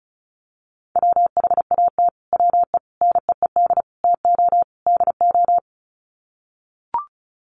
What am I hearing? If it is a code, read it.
Morse code, heard as a keyed tone, about 34 wpm, 700 Hz. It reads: WHAT WE NEED TO DO
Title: Sets of 5 Words Encore 35wpm